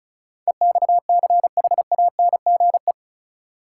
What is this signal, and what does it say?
Morse code, heard as a keyed tone, about 36 wpm, 700 Hz. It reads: EXCHANGE
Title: Sentences from Top 1000 Words - Review Part 6 - New Word Drill - 35wpm